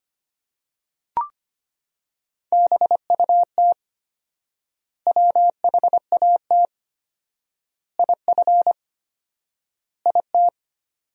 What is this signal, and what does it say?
Morse code, heard as a keyed tone, about 25 wpm, 700 Hz. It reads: BUT WHAT IF IT
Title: 4x Word-Spacing - Sentences from Top 100 Words - 25wpm